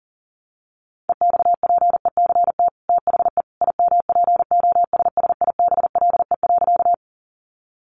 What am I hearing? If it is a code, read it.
Morse code, heard as a keyed tone, about 40 wpm, 700 Hz. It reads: EXPECT THE IMPOSSIBLE.
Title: Sentences from Top 1100 Words 40wpm